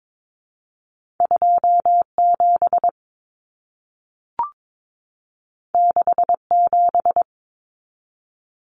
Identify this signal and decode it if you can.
Morse code, heard as a keyed tone, about 22 wpm, 700 Hz. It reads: 27 67
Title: Numbers - 2-Digit 22wpm